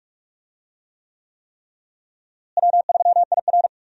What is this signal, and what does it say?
Morse code, heard as a keyed tone, about 46 wpm, 700 Hz. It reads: W3IR
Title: USA Call Signs - 4 Characters - Repeated 3x - 45wpm